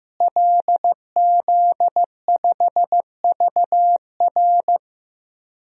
Morse code, heard as a keyed tone, about 15 wpm, 700 Hz. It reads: LZ5VR
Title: DXCC-100 Call Signs - 5 Characters - Repeated 2x - 15wpm